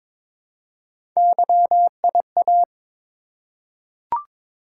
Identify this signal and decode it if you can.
Morse code, heard as a keyed tone, about 22 wpm, 700 Hz. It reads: YIA
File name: practice / 22wpm / Three Letter-Number Combinations - 22wpm